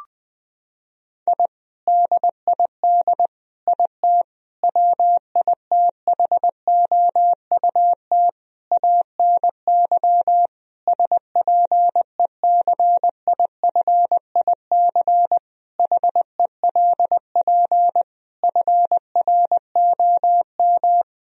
Morse code, heard as a keyed tone, about 21 wpm, 700 Hz. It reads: I DID IT WITHOUT ANY SPECIFIC HELP FROM
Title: Sentences from Top 1000 Words - Review Part 3 - 20wpm